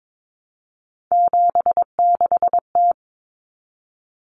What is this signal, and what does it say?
Morse code, heard as a keyed tone, about 22 wpm, 700 Hz. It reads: 76T
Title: Common Cut Numbers - 3-Digit 22wpm